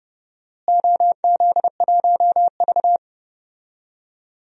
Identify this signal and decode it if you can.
Morse code, heard as a keyed tone, about 31 wpm, 700 Hz. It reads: OZ1V
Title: DXCC-100 Call Signs - 4 Characters - Repeated 2x - 30wpm